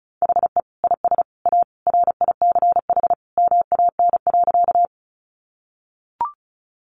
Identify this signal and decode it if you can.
Morse code, heard as a keyed tone, about 34 wpm, 700 Hz. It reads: HE IS A RICH MAN.
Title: Sentences from Top 1100 Words 35wpm